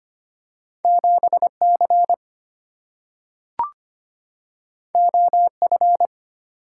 Morse code, heard as a keyed tone, about 25 wpm, 700 Hz. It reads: 7C OF
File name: Two Letter-Number Combinations - 25wpm